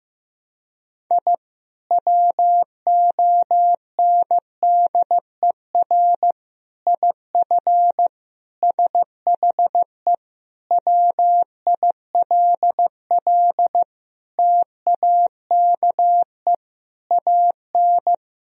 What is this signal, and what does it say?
Morse code, heard as a keyed tone, about 15 wpm, 700 Hz. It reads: I WONDER IF SHE WILL TAKE AN
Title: Sentences from Top 1000 Words - Review Part 3 - 15wpm